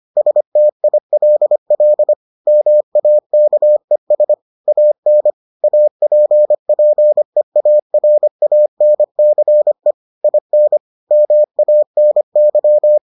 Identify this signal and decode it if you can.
Morse code, heard as a keyed tone, about 25 wpm, 600 Hz. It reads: STILL MAKES AN APPEARANCE IN MANY